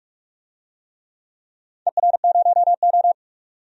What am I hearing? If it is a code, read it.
Morse code, heard as a keyed tone, about 46 wpm, 700 Hz. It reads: ER0O
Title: DXCC-100 Call Signs - 4 Characters - 45wpm